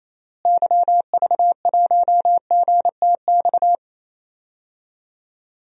Morse code, heard as a keyed tone, about 28 wpm, 700 Hz. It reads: YV1GTX
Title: DXCC-100 Call Signs - 6 Characters - Repeated 2x - 28wpm